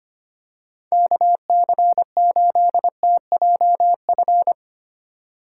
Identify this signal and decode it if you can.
Morse code, heard as a keyed tone, about 25 wpm, 700 Hz. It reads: KC8TJF
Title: USA Call Signs - 6 Characters - 2x Character Spacing - 25wpm